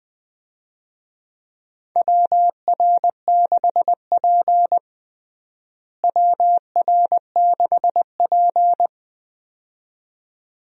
Morse code, heard as a keyed tone, about 19 wpm, 700 Hz. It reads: WR6P WR6P
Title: USA Call Signs - 4 Characters - Repeated 2x - 20wpm